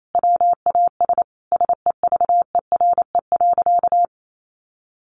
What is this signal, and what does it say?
Morse code, heard as a keyed tone, about 28 wpm, 700 Hz. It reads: WAS SEVERE.
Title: Sentences from Top 1400 Words 28wpm